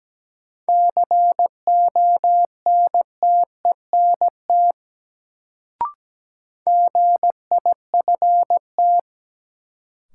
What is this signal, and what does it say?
Morse code, heard as a keyed tone, about 17 wpm, 700 Hz. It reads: CONTENT GIFT
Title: Sentences from Top 900 Words - New Word Drill - 17wpm